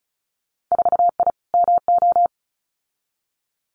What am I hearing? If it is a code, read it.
Morse code, heard as a keyed tone, about 36 wpm, 690 Hz. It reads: 4I MO
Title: Winter Field Day Exchanges 35wpm